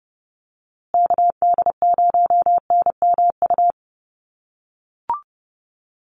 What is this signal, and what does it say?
Morse code, heard as a keyed tone, about 30 wpm, 700 Hz. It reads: KD0NMU
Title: USA Call Signs Repeated 2x 30wpm